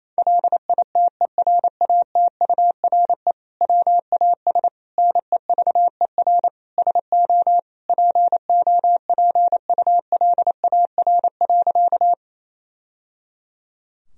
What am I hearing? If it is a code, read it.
Morse code, heard as a keyed tone, about 28 wpm, 700 Hz. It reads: LITERATURE WAS NEVER SO POPULAR.
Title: Sentences from Top 1000 Words - Review Part 3 - 28wpm